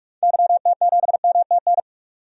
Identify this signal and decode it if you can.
Morse code, heard as a keyed tone, about 46 wpm, 700 Hz. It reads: YT7MTN
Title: DXCC-100 Call Signs - 6 Characters - 45wpm